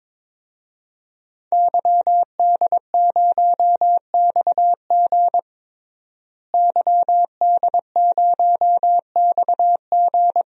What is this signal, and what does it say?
Morse code, heard as a keyed tone, about 21 wpm, 700 Hz. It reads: YD0XG YD0XG
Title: DXCC-100 Call Signs - 5 Characters - Repeated 2x - 22wpm